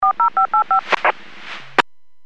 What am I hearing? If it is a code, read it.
DTMF, heard as telephone touch tones, tones of about 110 ms. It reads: 40585
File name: WPS / 00926 12-34 WPS 451.25 WPS